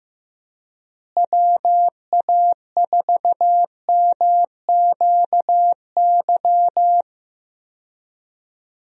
Morse code, heard as a keyed tone, about 15 wpm, 700 Hz. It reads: WA4MQY
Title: USA Call Signs - 6 Characters - Repeated 3x - 15wpm